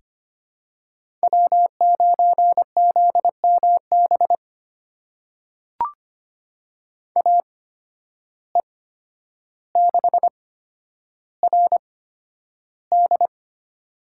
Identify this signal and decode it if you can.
Morse code, heard as a keyed tone, about 25 wpm, spaced as for 11 wpm, 700 Hz. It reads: W9ZMB AE6RD
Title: USA Call Signs - 5 Characters - 8x Character Spacing - 25wpm